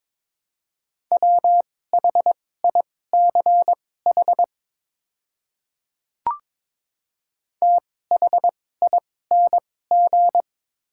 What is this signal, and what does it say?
Morse code, heard as a keyed tone, about 22 wpm, spaced as for 16 wpm, 700 Hz. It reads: WHICH THING
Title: 2x Character Spacing - Top 100 Words - 22wpm